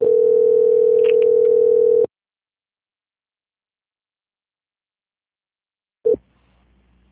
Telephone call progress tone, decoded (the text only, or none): ringback tone